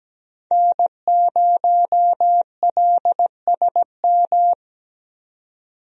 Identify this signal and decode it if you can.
Morse code, heard as a keyed tone, about 17 wpm, 700 Hz. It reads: N0LSM
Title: USA Call Signs - 5 Characters - 17wpm